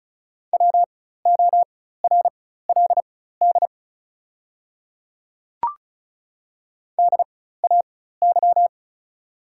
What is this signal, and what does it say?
Morse code, heard as a keyed tone, about 36 wpm, spaced as for 16 wpm, 700 Hz. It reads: WORLD DAY
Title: 4x Character Spacing - Top 200 Words - 35wpm